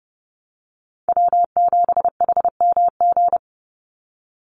Morse code, heard as a keyed tone, about 30 wpm, 700 Hz. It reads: W7HMG
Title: USA Call Signs 30wpm